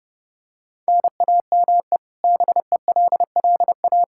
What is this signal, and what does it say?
Morse code, heard as a keyed tone, about 30 wpm, 700 Hz. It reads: NAME BELLA
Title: Practice QSO - Exchange 1 - Beginning - 30wpm